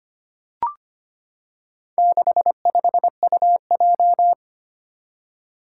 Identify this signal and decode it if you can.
Morse code, heard as a keyed tone, about 25 wpm, 700 Hz. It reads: 65UJ
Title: Four Challenging Letter-Number Combinations 25wpm